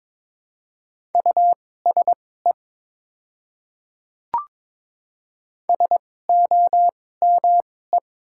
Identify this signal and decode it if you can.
Morse code, heard as a keyed tone, about 22 wpm, spaced as for 16 wpm, 700 Hz. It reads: USE SOME
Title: 2x Character Spacing - Top 100 Words - 22wpm